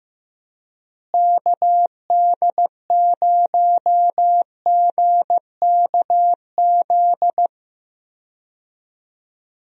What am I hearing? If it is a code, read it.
Morse code, heard as a keyed tone, about 15 wpm, 700 Hz. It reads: KD0GKZ
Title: USA Call Signs - 6 Characters - Repeated 3x - 15wpm